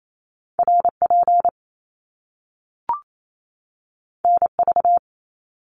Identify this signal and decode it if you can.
Morse code, heard as a keyed tone, about 28 wpm, 700 Hz. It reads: RP NV
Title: Top Two Letter Combinations in Words 28wpm